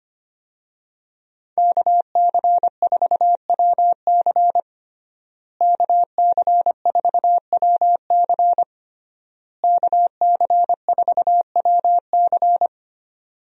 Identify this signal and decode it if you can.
Morse code, heard as a keyed tone, about 25 wpm, 700 Hz. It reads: KC4WC KC4WC KC4WC
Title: USA Call Signs - 5 Characters - Repeated 3x - 25wpm